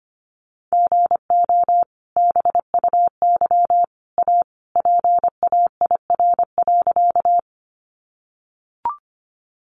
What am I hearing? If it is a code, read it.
Morse code, heard as a keyed tone, about 25 wpm, 700 Hz. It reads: GO BUY A PAIR.
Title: Sentences from Top 1200 Words 25wpm